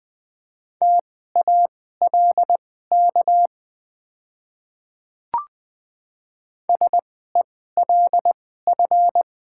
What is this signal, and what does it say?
Morse code, heard as a keyed tone, about 21 wpm, spaced as for 15 wpm, 700 Hz. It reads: TALK SELF
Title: 2x Character Spacing - Top 500 Words - 20wpm